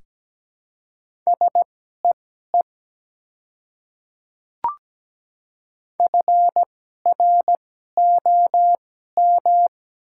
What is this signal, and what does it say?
Morse code, heard as a keyed tone, about 17 wpm, spaced as for 12 wpm, 700 Hz. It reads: SEE FROM
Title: 2x Character Spacing - Top 100 Words - 17wpm